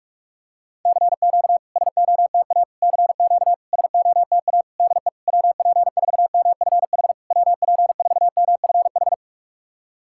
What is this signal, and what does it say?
Morse code, heard as a keyed tone, about 46 wpm, 690 Hz. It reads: CQ SOTA CQ SOTA DE WP4MFH WP4MFH
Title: SOTA QSO Practice 45wpm